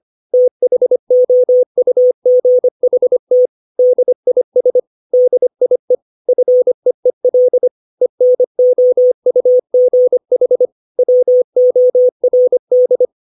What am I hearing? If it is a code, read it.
Morse code, heard as a keyed tone, about 25 wpm, 500 Hz. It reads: THOUGHT DIS DIE FEEL ENOUGH WORD
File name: 632 words / 632 Words 25WPM